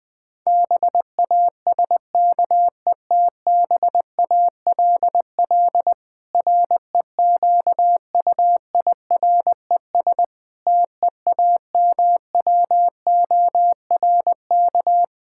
Morse code, heard as a keyed tone, about 19 wpm, 700 Hz. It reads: BASKETBALL REQUIRES TEAMWORK
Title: Difficult Sentences with Hints - 20wpm